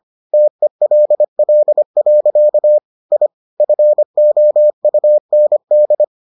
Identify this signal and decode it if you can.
Morse code, heard as a keyed tone, about 25 wpm, 600 Hz. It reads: TELL. I FOUND